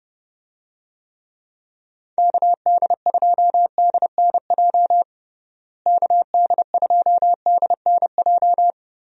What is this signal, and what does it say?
Morse code, heard as a keyed tone, about 30 wpm, 700 Hz. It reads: KD2DNJ KD2DNJ
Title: USA Call Signs - 6 Characters - Repeated 2x - 30wpm